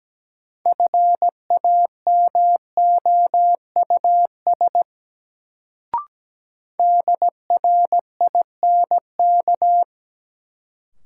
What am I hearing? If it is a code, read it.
Morse code, heard as a keyed tone, about 17 wpm, 700 Hz. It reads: FAMOUS DRINK
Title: Sentences from Top 1500 Words - New Word Drill - 17wpm